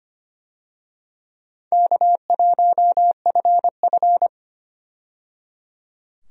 Morse code, heard as a keyed tone, about 25 wpm, 700 Hz. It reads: K1FF
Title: USA Call Signs - 4 Characters - 25wpm